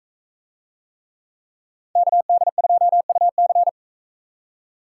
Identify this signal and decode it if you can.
Morse code, heard as a keyed tone, about 44 wpm, 700 Hz. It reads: KD2UC